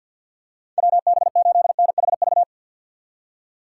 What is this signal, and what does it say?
Morse code, heard as a keyed tone, about 49 wpm, 700 Hz. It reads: WB8NHV
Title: USA Call Signs - 6 Characters - 3x Character Spacing - 50wpm